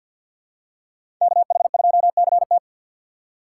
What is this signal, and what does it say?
Morse code, heard as a keyed tone, about 52 wpm, 700 Hz. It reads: KH2CT
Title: USA Call Signs - 5 Characters - 2x Character Spacing - 50wpm